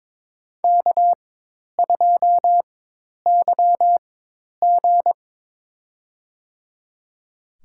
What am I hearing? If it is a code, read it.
Morse code, heard as a keyed tone, about 21 wpm, spaced as for 10 wpm, 700 Hz. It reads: K2YG
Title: USA Call Signs - 4 Characters - 4x Character Spacing - 22wpm